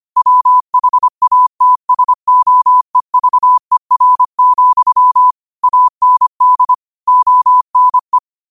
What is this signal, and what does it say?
Morse code, heard as a keyed tone, about 25 wpm, 1000 Hz. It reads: WHATSOEVER, AND ONE